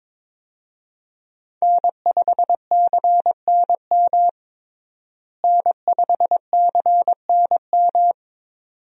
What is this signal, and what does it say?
Morse code, heard as a keyed tone, about 22 wpm, 700 Hz. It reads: N5CNM N5CNM
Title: USA Call Signs - 5 Characters - Repeated 2x - 22wpm